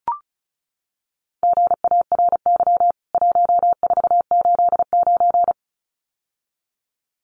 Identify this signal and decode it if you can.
Morse code, heard as a keyed tone, about 34 wpm, 700 Hz. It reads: GARY 1489
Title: CWT Exchange 35wpm